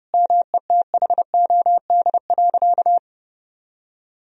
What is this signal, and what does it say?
Morse code, heard as a keyed tone, about 32 wpm, 700 Hz. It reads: METHOD.
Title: Sentences from Top 1000 Words - Review Part 3 - 30wpm